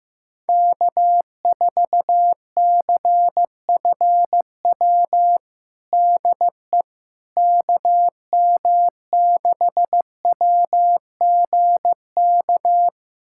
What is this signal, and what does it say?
Morse code, heard as a keyed tone, about 15 wpm, 700 Hz. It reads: K4CFW DE KM6WGK
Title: USA Callsign DE Callsign 15wpm